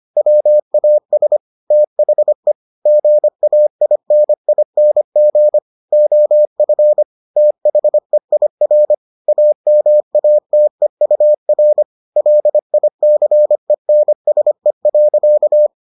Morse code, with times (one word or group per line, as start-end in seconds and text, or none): 0.2-1.4 WAS
1.7-2.5 THE
2.9-5.6 GAINING
5.9-7.0 OF
7.4-8.9 THEIR
9.3-11.8 AMATEUR
12.2-15.7 LICENSE.